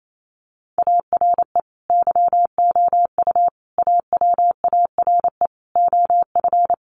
Morse code, heard as a keyed tone, about 28 wpm, 700 Hz. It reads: ARE YOU AWARE OF
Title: Sentences from Top 1500 Words 28wpm